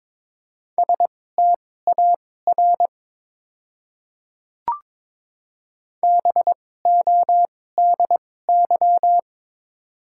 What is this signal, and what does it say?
Morse code, heard as a keyed tone, about 22 wpm, spaced as for 16 wpm, 690 Hz. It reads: STAR BODY
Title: 2x Character Spacing - Top 500 Words - 22wpm